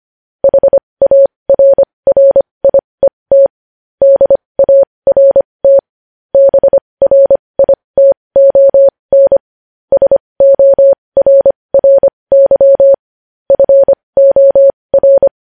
Morse code, heard as a keyed tone, about 25 wpm, spaced as for 20 wpm, 550 Hz. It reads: HARRIET DART BRITON SORRY FOR
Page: news-headlines-in-morse-code-at-20-wpm